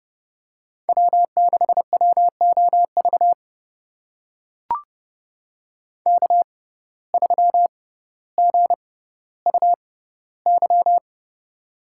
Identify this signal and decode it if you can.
Morse code, heard as a keyed tone, about 31 wpm, spaced as for 16 wpm, 700 Hz. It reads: W6WOV K3GUY
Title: USA Call Signs - 5 Characters - 6x Character Spacing - 30wpm